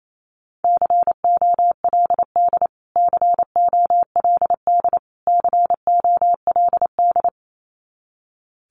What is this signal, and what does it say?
Morse code, heard as a keyed tone, about 28 wpm, 700 Hz. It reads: COLD COLD COLD
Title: Top 500 Words Repeated 28wpm